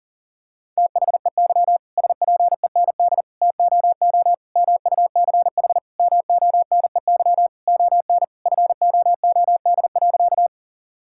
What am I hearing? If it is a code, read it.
Morse code, heard as a keyed tone, about 40 wpm, 700 Hz. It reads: THEY SPEND TOO MUCH MONEY ON FOOD.
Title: Sentences from Top 1000 Words - Review Part 5 - 40wpm